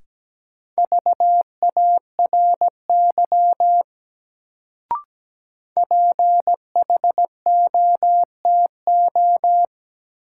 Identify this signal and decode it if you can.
Morse code, heard as a keyed tone, about 17 wpm, 700 Hz. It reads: VARY PHOTO
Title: Sentences from Top 1200 Words - New Word Drill - 17wpm